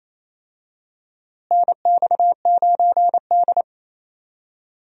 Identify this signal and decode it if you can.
Morse code, heard as a keyed tone, about 27 wpm, 700 Hz. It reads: NX9D
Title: USA Call Signs - 4 Characters - Repeated 2x - 28wpm